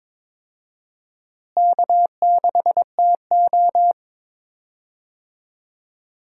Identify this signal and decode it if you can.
Morse code, heard as a keyed tone, about 22 wpm, 700 Hz. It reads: K6TO
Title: USA Call Signs - 4 Characters - Repeated 2x - 22wpm